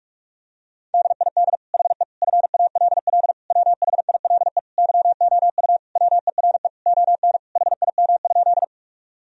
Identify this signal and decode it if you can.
Morse code, heard as a keyed tone, about 46 wpm, 700 Hz. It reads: DID HE FALL WHILE YOU WERE ON HIM?
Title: Sentences from Top 1000 Words - Review Part 2 - 45wpm